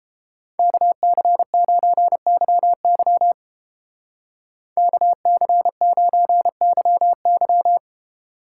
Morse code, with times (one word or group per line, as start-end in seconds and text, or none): 0.6-3.3 KC9YY
4.8-7.8 KC9YY